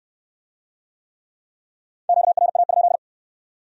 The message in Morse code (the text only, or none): QRA?